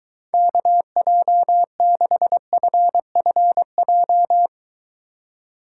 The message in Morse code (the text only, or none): KJ6FFJ